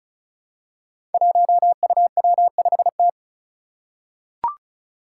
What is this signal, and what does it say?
Morse code, heard as a keyed tone, about 36 wpm, 700 Hz. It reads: 1UW5T
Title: Five Challenging Letter-Number Combinations 35wpm